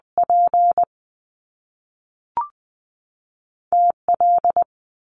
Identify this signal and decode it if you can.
Morse code, heard as a keyed tone, about 20 wpm, 700 Hz. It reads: P TL
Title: Top Two Letter Combinations in Words 20wpm